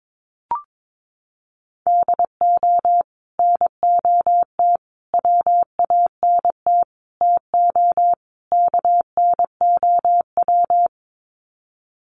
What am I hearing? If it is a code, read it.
Morse code, heard as a keyed tone, about 22 wpm, 700 Hz. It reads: DO NOT WANT TO KNOW
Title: Sets of 5 Words Encore 22wpm